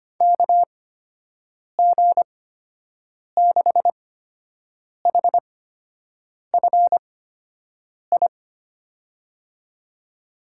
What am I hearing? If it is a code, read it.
Morse code, heard as a keyed tone, about 25 wpm, spaced as for 7 wpm, 700 Hz. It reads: KG6HFI
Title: USA Call Signs - 6 Characters - 8x Character Spacing - 25wpm